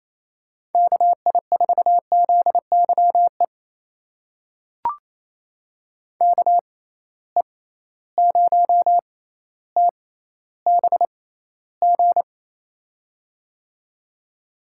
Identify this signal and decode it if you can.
Morse code, heard as a keyed tone, about 28 wpm, spaced as for 14 wpm, 700 Hz. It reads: KI4ZYE KE0TBG
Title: USA Call Signs - 6 Characters - 6x Character Spacing - 28wpm